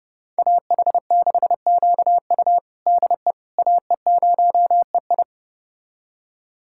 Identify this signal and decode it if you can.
Morse code, heard as a keyed tone, about 32 wpm, 700 Hz. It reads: AH6QU DE AE0EI
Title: USA Callsign DE Callsign - 5 Characters - 30wpm